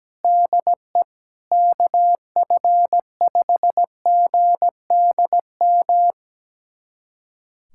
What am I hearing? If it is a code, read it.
Morse code, heard as a keyed tone, about 17 wpm, 700 Hz. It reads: DE KF5GDM
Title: USA Callsign DE Callsign - 6 Characters - 17wpm